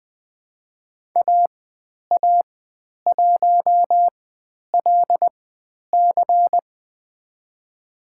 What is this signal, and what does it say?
Morse code, heard as a keyed tone, about 20 wpm, spaced as for 10 wpm, 700 Hz. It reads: AA1LC
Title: 095 Lesson 16g - Letter c - Call Signs 20wpm